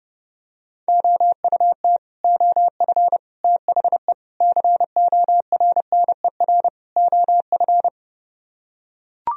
Morse code, heard as a keyed tone, about 30 wpm, 700 Hz. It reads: OUT OF THE CORNER OF